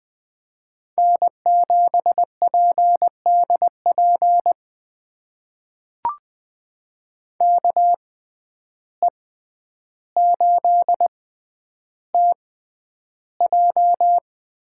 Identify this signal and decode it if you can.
Morse code, heard as a keyed tone, about 20 wpm, spaced as for 10 wpm, 700 Hz. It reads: N7PDP KE8TJ
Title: USA Call Signs - 5 Characters - 6x Character Spacing - 20wpm